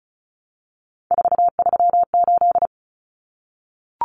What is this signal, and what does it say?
Morse code, heard as a keyed tone, about 34 wpm, 700 Hz. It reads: 438
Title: Common Cut Numbers - 3-Digit 35wpm